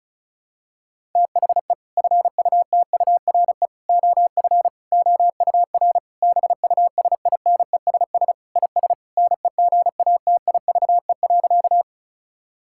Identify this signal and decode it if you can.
Morse code, heard as a keyed tone, about 36 wpm, 700 Hz. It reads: THE FUTURE OF OUR BUSINESS IS NEGATIVE.
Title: Sentences from Top 1000 Words - Review Part 6 - 35wpm